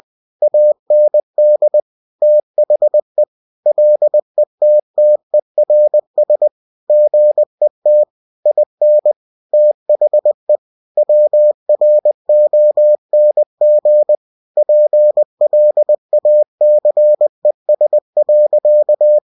Morse code, heard as a keyed tone, about 20 wpm, 600 Hz. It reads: AND THE LETTERS GET IN THE WRONG PLACES.
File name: cw-books / winnie-the-pooh / winnie-20wpm-page-0051